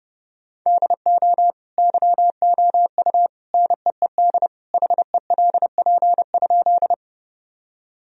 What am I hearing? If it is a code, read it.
Morse code, heard as a keyed tone, about 32 wpm, 700 Hz. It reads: DO YOU NEED HELP?
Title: Sentences from Top 1000 Words - Review Part 1 - 30wpm